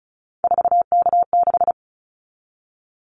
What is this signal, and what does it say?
Morse code, heard as a keyed tone, about 34 wpm, 700 Hz. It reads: 4K6
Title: Callsign Prefixes for Top 100 DXCC - 35wpm